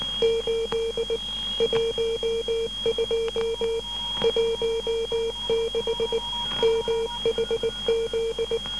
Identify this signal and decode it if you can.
Morse code, heard as a keyed tone, about 19 wpm, 460 Hz, with noise. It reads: 8 12 16 MHZ
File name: SAG6 12880.5 1218